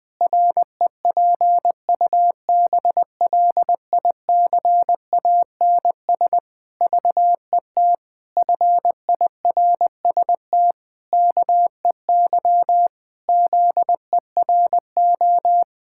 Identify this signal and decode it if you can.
Morse code, heard as a keyed tone, about 20 wpm, 700 Hz. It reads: REPUBLICANS VET FIRST KEY ZERO
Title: Pangram Challenge v2 - 20wpm